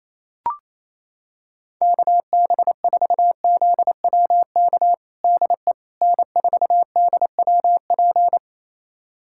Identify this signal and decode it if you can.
Morse code, heard as a keyed tone, about 28 wpm, 700 Hz. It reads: KB4ZWK DE N4DWP
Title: USA Callsign DE Callsign 28wpm